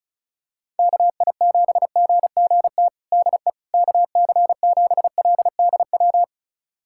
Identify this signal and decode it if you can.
Morse code, heard as a keyed tone, about 34 wpm, 700 Hz. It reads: KI7GGT DE KC7LDW
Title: USA Callsign DE Callsign - 6 Characters - 35wpm